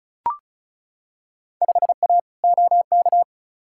Morse code, heard as a keyed tone, about 36 wpm, 700 Hz. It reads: 5A OK
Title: Field Day Exchanges - 35wpm